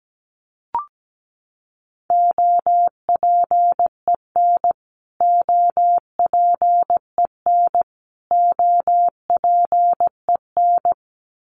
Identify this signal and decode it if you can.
Morse code, heard as a keyed tone, about 17 wpm, 700 Hz. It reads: OPEN OPEN OPEN
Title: Top 500 Words Repeated 17wpm